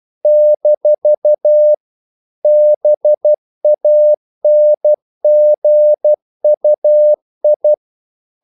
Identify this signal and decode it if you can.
Morse code, heard as a keyed tone, about 12 wpm, 600 Hz. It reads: - BANGUI
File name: CW / Practice / NationalCapitals12